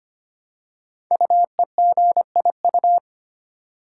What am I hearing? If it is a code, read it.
Morse code, heard as a keyed tone, about 25 wpm, 700 Hz. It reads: UEGIU
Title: Code Groups - 5 Letters - 25wpm